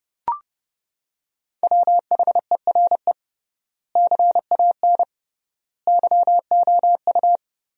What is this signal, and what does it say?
Morse code, heard as a keyed tone, about 30 wpm, 700 Hz. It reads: WHERE CAN YOU
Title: 3x Word-Spacing - Sentences from Top 500 Words - 30wpm